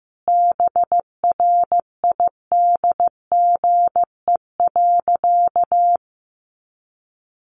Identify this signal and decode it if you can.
Morse code, heard as a keyed tone, about 15 wpm, 700 Hz. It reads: BRIDGE.
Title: Sentences from Top 1200 Words 15wpm